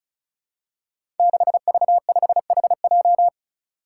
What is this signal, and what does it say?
Morse code, heard as a keyed tone, about 34 wpm, 700 Hz. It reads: 6V5HJ
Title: Five Challenging Letter-Number Combinations 35wpm